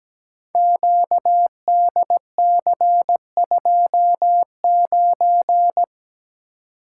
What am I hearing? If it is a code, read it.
Morse code, heard as a keyed tone, about 17 wpm, 700 Hz. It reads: QDC29
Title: Code Groups - 5 Letter:Numbers - 17wpm